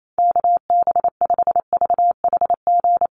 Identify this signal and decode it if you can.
Morse code, heard as a keyed tone, about 28 wpm, 700 Hz. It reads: KB5VHG
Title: USA Call Signs Repeated 3x 28wpm